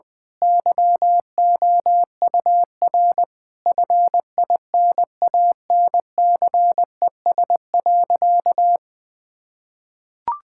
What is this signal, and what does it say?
Morse code, heard as a keyed tone, about 20 wpm, 700 Hz. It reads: YOUR FINANCES.
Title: Difficult Sentences with Hints - 20wpm